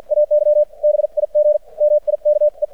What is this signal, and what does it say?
Morse code, heard as a keyed tone, about 50 wpm, spaced as for 43 wpm, 600 Hz. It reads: AY BIG MIKE
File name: archived / CRSnet